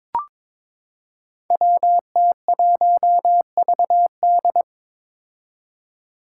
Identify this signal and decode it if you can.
Morse code, heard as a keyed tone, about 22 wpm, 700 Hz. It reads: WT1VD
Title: Five Challenging Letter-Number Combinations 22wpm